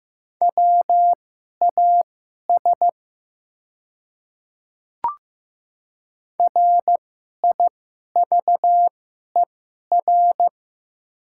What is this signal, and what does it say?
Morse code, heard as a keyed tone, about 15 wpm, spaced as for 11 wpm, 700 Hz. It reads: WAS RIVER
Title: 2x Character Spacing - Top 300 Words - 15wpm